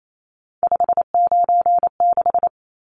Morse code, heard as a keyed tone, about 28 wpm, 700 Hz. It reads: 596
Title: Numbers - 3-Digit 28wpm